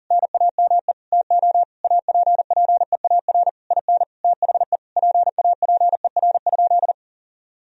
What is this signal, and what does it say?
Morse code, heard as a keyed tone, about 37 wpm, 700 Hz. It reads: NAME TO APPEAR IN THE PAPER?
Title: Sentences from Top 1000 Words - Review Part 2 - 40wpm